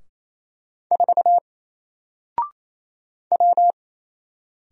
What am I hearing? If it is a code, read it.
Morse code, heard as a keyed tone, about 27 wpm, 700 Hz. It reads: 4 W
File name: Single-Letter-Number-28wpm